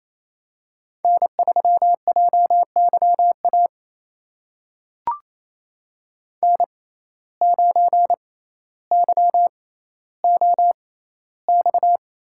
Morse code, heard as a keyed tone, about 28 wpm, spaced as for 14 wpm, 700 Hz. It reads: N3JYA N9YOX
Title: USA Call Signs - 5 Characters - 6x Character Spacing - 28wpm